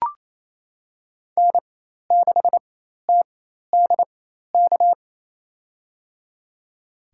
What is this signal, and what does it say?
Morse code, heard as a keyed tone, about 27 wpm, spaced as for 13 wpm, 700 Hz. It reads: N6TDK
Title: USA Call Signs - 4x Character Spacing - 28wpm